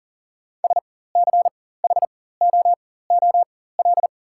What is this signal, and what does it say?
Morse code, heard as a keyed tone, about 40 wpm, spaced as for 19 wpm, 700 Hz. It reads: SCHOOL
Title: 4x Character Spacing - Top 300 Words - 40wpm